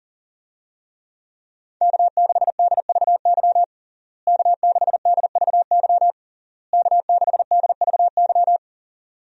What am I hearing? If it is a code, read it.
Morse code, heard as a keyed tone, about 40 wpm, 700 Hz. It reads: K6DVY K6DVY K6DVY
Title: USA Call Signs - 5 Characters - Repeated 3x - 40wpm